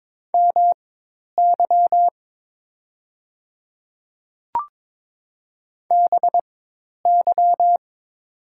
Morse code, heard as a keyed tone, about 22 wpm, spaced as for 10 wpm, 700 Hz. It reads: MY BY
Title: 4x Character Spacing - Top 200 Words - 22wpm